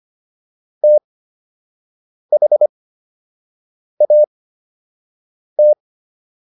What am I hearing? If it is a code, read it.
Morse code, heard as a keyed tone, about 25 wpm, spaced as for 6 wpm, 600 Hz. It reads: THAT